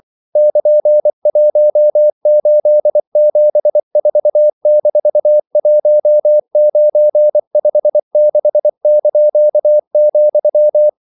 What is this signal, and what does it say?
Morse code, heard as a keyed tone, about 24 wpm, 600 Hz. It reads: <KN>1874-1956),
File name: CW / Practice / EnglishQuotations0424